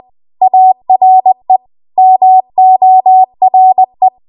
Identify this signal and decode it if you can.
Morse code, heard as a keyed tone, about 20 wpm, 750 Hz. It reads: ARE MORE
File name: Morse / 20WPM